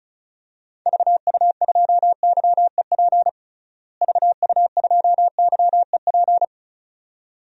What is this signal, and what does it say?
Morse code, heard as a keyed tone, about 36 wpm, 700 Hz. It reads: VU2YEP VU2YEP
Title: DXCC-100 Call Signs - 6 Characters - Repeated 2x - 35wpm